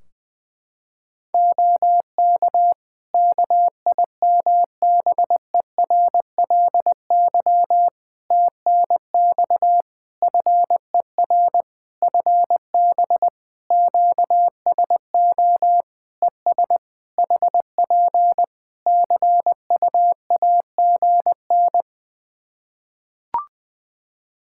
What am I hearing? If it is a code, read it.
Morse code, heard as a keyed tone, about 20 wpm, 700 Hz. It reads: OK KIMBERLY TNX FER FB QSO ES HP CUAGN
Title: Practice QSO - Exchange 4 - Ending - 20wpm